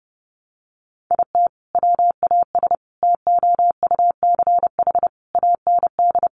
Morse code, heard as a keyed tone, about 30 wpm, 700 Hz. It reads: IT WAS TOUCH AND
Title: Sentences from Top 1200 Words 30wpm